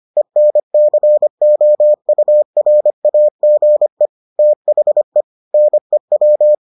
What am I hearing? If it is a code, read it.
Morse code, heard as a keyed tone, about 25 wpm, 600 Hz. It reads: ENCOURAGE THE NEW